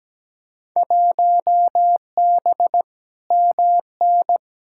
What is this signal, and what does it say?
Morse code, heard as a keyed tone, about 17 wpm, 700 Hz. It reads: 1B MN
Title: Field Day Exchanges Encore - 17wpm